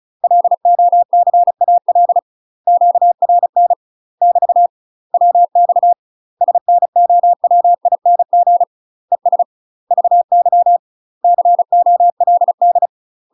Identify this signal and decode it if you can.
Morse code, heard as a keyed tone, about 36 wpm, 700 Hz. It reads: LOCAL QRN <BT> WX SNOWING ES VY COLD
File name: UK MASTER QSOs TRAINING / 35WPM / UK MASTER QSOs 0023